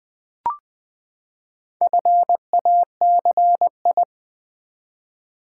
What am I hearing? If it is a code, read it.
Morse code, heard as a keyed tone, about 20 wpm, 700 Hz. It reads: FACI
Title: Build A Word - 4 Letters At A Time - 10-Letter Words - 20wpm